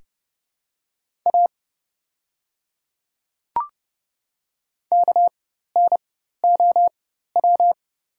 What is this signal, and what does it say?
Morse code, heard as a keyed tone, about 29 wpm, spaced as for 14 wpm, 700 Hz. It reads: A KNOW
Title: 4x Character Spacing - Top 100 Words - 30wpm